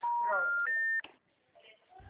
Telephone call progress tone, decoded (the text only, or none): special information tone